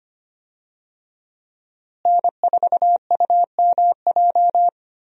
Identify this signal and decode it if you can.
Morse code, heard as a keyed tone, about 25 wpm, 700 Hz. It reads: N4UMJ
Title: USA Call Signs - 5 Characters - Repeated 2x - 25wpm